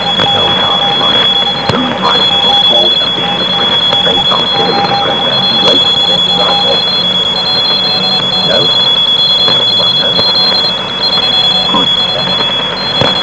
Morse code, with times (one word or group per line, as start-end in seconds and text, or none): none